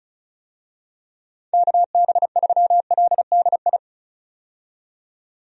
Morse code, heard as a keyed tone, about 34 wpm, 700 Hz. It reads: KB3LDI
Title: USA Call Signs - 6 Characters - 35wpm